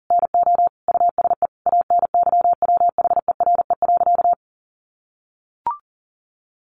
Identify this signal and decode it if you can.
Morse code, heard as a keyed tone, about 40 wpm, 700 Hz. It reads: NO USE ANYWHERE.
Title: Sentences from Top 1400 Words 40wpm